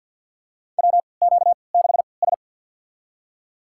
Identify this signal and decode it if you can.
Morse code, heard as a keyed tone, about 51 wpm, spaced as for 28 wpm, 700 Hz. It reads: WQ6S
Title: USA Call Signs - 4 Characters - 3x Character Spacing - 50wpm